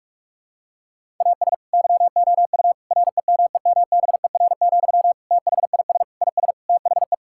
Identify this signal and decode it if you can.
Morse code, heard as a keyed tone, about 46 wpm, 700 Hz. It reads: AS YOU REMEMBER, THIS IS THE
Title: Sentences from Top 1000 Words - Review Part 5 - 45wpm